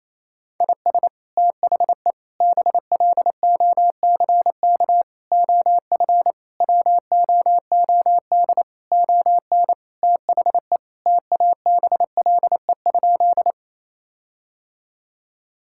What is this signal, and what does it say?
Morse code, heard as a keyed tone, about 28 wpm, 700 Hz. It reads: IS THE BLOCK OF WOOD ON THE TABLE?
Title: Sentences from Top 1600 Words 28wpm